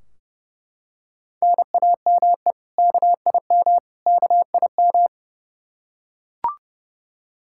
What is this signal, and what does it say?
Morse code, heard as a keyed tone, about 31 wpm, 700 Hz. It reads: NAME KIM KIM
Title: K1USN SST QSO Practice 30wpm